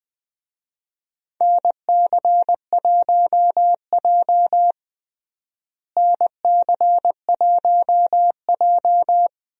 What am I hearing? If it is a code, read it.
Morse code, heard as a keyed tone, about 20 wpm, 700 Hz. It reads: NC1J NC1J
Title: USA Call Signs - 4 Characters - Repeated 3x - 20wpm